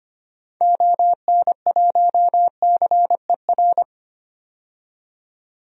Morse code, heard as a keyed tone, about 25 wpm, 700 Hz. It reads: ON1CER
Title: DXCC-100 Call Signs - 6 Characters - 25wpm